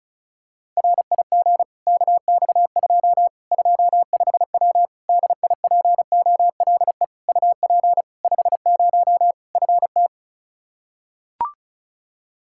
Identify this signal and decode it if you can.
Morse code, heard as a keyed tone, about 34 wpm, 700 Hz. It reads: RIG KX2 25W DIPOLE UP 50 FT
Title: Practice QSO - Exchange 2 - Info - 35wpm